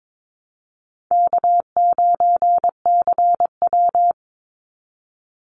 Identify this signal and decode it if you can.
Morse code, heard as a keyed tone, about 22 wpm, 700 Hz. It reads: K9CW
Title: Easy Call Signs - 22wpm